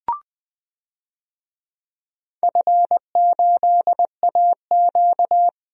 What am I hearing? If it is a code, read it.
Morse code, heard as a keyed tone, about 21 wpm, 700 Hz. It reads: F8AQ
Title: DXCC-100 Call Signs - 4 Characters - 20wpm